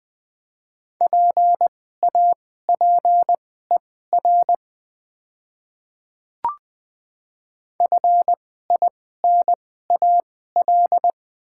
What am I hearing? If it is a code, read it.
Morse code, heard as a keyed tone, about 20 wpm, spaced as for 15 wpm, 690 Hz. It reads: PAPER FINAL
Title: 2x Character Spacing - Top 500 Words - 20wpm